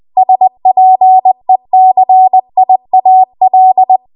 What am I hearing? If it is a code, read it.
Morse code, heard as a keyed tone, about 20 wpm, 750 Hz. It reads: SPECIAL
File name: Archive / 20 WPM / 20WPM